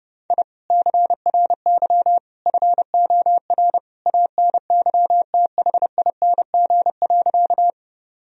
Morse code, heard as a keyed tone, about 30 wpm, 700 Hz. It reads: I CRY FOR ANYTHING.
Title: Sentences from Top 1000 Words - Review Part 2 - 30wpm